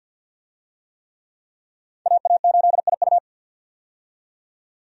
Morse code, heard as a keyed tone, about 52 wpm, 700 Hz. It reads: AA8IU
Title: USA Call Signs - 5 Characters - 50wpm